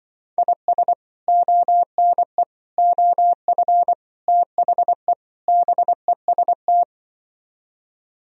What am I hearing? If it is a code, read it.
Morse code, heard as a keyed tone, about 24 wpm, 700 Hz. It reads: IS ONE OF THE BEST
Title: Speed-Racing - Sets of 5 Words 20wpm